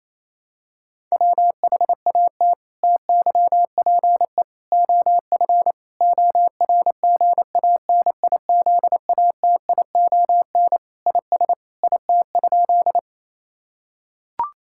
Morse code, heard as a keyed tone, about 28 wpm, 700 Hz. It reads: WHAT TYPE OF ORGANIZATION IS IT?
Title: Sentences from Top 1000 Words - Review Part 5 - 28wpm